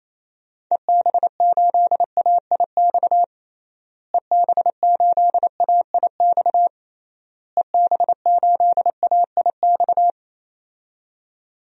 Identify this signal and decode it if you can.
Morse code, heard as a keyed tone, about 28 wpm, 700 Hz. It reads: EB8AIX EB8AIX EB8AIX
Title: DXCC-100 Call Signs - 6 Characters - Repeated 3x - 28wpm